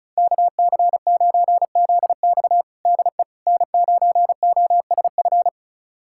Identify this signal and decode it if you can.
Morse code, heard as a keyed tone, about 34 wpm, 690 Hz. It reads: KC9ZX DE N9OSF
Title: USA Callsign DE Callsign - 5 Characters - 35wpm